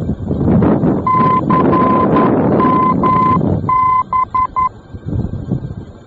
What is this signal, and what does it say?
Morse code, heard as a keyed tone, about 11 wpm, 1020 Hz, with noise. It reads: CMB
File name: france / mx / CMB-F-VOR